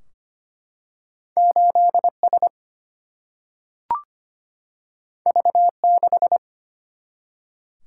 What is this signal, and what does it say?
Morse code, heard as a keyed tone, about 25 wpm, 700 Hz. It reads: 8 S V 6
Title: Single-Letter Single-Number 25wpm